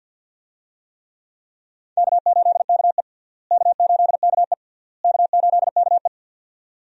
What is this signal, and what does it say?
Morse code, heard as a keyed tone, about 48 wpm, 700 Hz. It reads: K8KE K8KE K8KE
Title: USA Call Signs - 4 Characters - Repeated 3x - 50wpm